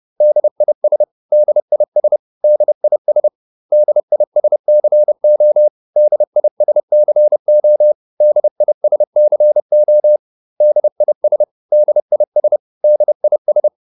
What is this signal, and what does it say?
Morse code, heard as a keyed tone, about 30 wpm, 600 Hz. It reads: DIS DIS DIS DISCO DISCO DISCO DIS DIS DIS
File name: DIS5 30